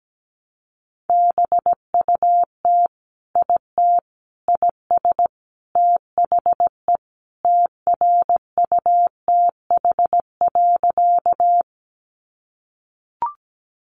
Morse code, heard as a keyed tone, about 17 wpm, 700 Hz. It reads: BUT IT IS THE TRUTH.
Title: Sentences from Top 800 Words 17wpm